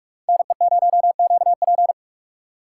Morse code, heard as a keyed tone, about 47 wpm, 700 Hz. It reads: NE0QP
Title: USA Call Signs - 5 Characters - 6x Character Spacing - 45wpm